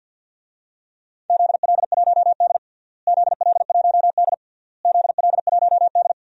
Morse code, heard as a keyed tone, about 50 wpm, 700 Hz. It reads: ZL1D ZL1D ZL1D
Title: DXCC-100 Call Signs - 4 Characters - Repeated 3x - 50wpm